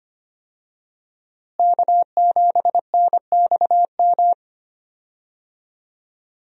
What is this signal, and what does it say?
Morse code, heard as a keyed tone, about 25 wpm, 700 Hz. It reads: K7NXM
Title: USA Call Signs - 5 Characters - Repeated 2x - 25wpm